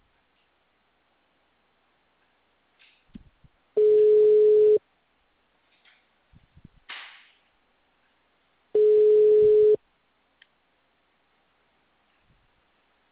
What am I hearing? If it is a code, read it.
Ringback tone (Europe, 425 Hz).